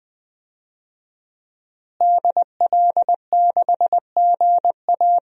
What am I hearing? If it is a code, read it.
Morse code, heard as a keyed tone, about 20 wpm, 700 Hz. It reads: DL6GA
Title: DXCC-100 Call Signs - 5 Characters - Repeated 2x - 20wpm